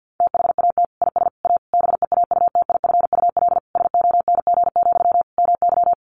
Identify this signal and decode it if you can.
Morse code, heard as a keyed tone, about 50 wpm, 700 Hz. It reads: THAT IS A BEAUTIFUL SONG, MY
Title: Sentences from Top 1500 Words 50wpm